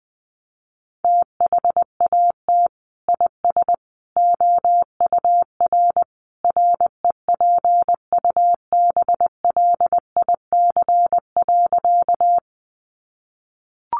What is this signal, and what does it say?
Morse code, heard as a keyed tone, about 20 wpm, 700 Hz. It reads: THAT IS OUR REPUBLIC.
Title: Sentences from Top 1500 Words 20wpm